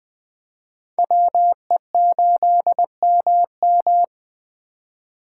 Morse code, heard as a keyed tone, about 20 wpm, 700 Hz. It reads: WE8MM
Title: Speed Racing - USA 2x2 Call Signs 17wpm